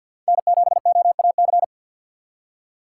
Morse code, heard as a keyed tone, about 50 wpm, 700 Hz. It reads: N7OAC
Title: USA Call Signs - 5 Characters - 2x Character Spacing - 50wpm